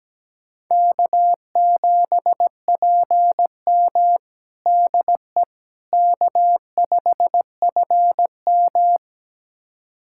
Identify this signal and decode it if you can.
Morse code, heard as a keyed tone, about 17 wpm, 700 Hz. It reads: K7PM DE K5FM
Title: USA Callsign DE Callsign - 4 Characters - 17wpm